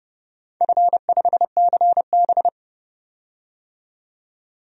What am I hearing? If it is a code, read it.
Morse code, heard as a keyed tone, about 30 wpm, 700 Hz. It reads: F5CB
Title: DXCC-100 Call Signs - 4 Characters - 30wpm